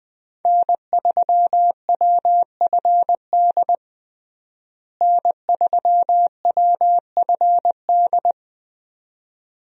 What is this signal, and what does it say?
Morse code, heard as a keyed tone, about 21 wpm, 700 Hz. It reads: N3WFD N3WFD
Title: USA Call Signs - 5 Characters - Repeated 2x - 20wpm